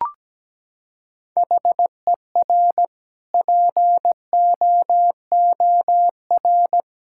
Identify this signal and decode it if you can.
Morse code, heard as a keyed tone, about 17 wpm, 700 Hz. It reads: HER POOR
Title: Sentences from Top 1000 Words - Review Part 6 - 17wpm